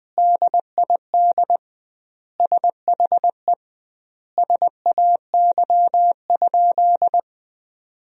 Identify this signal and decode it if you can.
Morse code, heard as a keyed tone, about 20 wpm, 700 Hz. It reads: DID SHE SAY?
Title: 2x Word-Spacing - Sentences from Top 200 Words - 20wpm